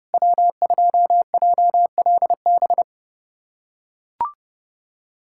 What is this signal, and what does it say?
Morse code, heard as a keyed tone, about 30 wpm, 700 Hz. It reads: W2JLB
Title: USA Callsign DE Callsign - 5 Characters - 30wpm